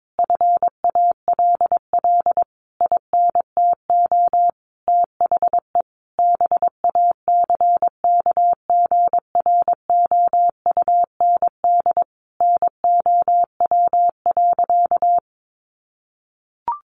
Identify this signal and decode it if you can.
Morse code, heard as a keyed tone, about 22 wpm, 700 Hz. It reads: FALL INTO THE BACKGROUND NOW.
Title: Sentences from Top 1300 Words 22wpm